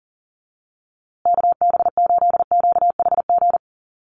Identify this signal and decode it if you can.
Morse code, heard as a keyed tone, about 40 wpm, 700 Hz. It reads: KB8QHG
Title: USA Call Signs Repeated 2x 40wpm